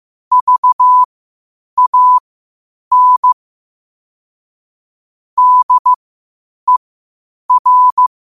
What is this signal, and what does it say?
Morse code, heard as a keyed tone, about 15 wpm, spaced as for 8 wpm, 1000 Hz. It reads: VAN DER
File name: qotdfiles / 8wpm